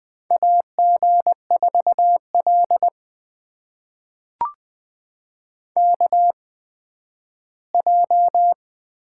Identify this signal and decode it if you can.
Morse code, heard as a keyed tone, about 20 wpm, spaced as for 12 wpm, 700 Hz. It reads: AG4L KJ
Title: USA Call Signs - 4 Characters - 8x Character Spacing - 20wpm